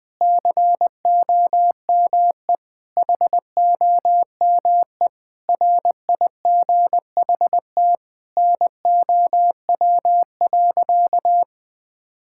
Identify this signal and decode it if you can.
Morse code, heard as a keyed tone, about 20 wpm, 700 Hz. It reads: COME HOME RIGHT NOW.
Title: Sentences with 5-Word Phrases - 20wpm